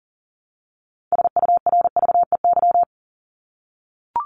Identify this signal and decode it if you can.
Morse code, heard as a keyed tone, about 40 wpm, 700 Hz. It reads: SURVEY
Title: Top 1501-1750 Words 40wpm